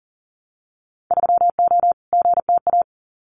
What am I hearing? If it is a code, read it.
Morse code, heard as a keyed tone, about 40 wpm, 700 Hz. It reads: 3O GTA
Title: Winter Field Day Exchanges Encore 40wpm